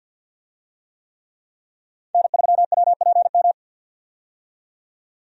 Morse code, heard as a keyed tone, about 50 wpm, 700 Hz. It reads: N3WPM
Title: USA Call Signs - 5 Characters - 50wpm